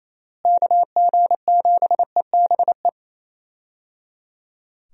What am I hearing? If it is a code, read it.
Morse code, heard as a keyed tone, about 28 wpm, 700 Hz. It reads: KG7EBE
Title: USA Call Signs - 6 Characters - Repeated 3x - 28wpm